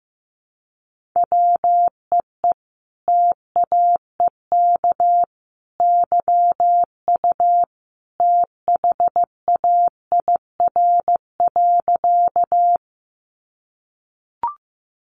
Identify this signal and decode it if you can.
Morse code, heard as a keyed tone, about 15 wpm, 700 Hz. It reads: WEE TAEK YU THAIR.
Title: A Sound Of Thunder 15wpm